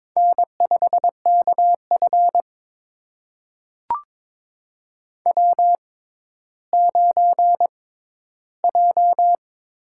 Morse code, heard as a keyed tone, about 22 wpm, spaced as for 12 wpm, 700 Hz. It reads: N5KF W9J
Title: USA Call Signs - 4 Characters - 6x Character Spacing - 22wpm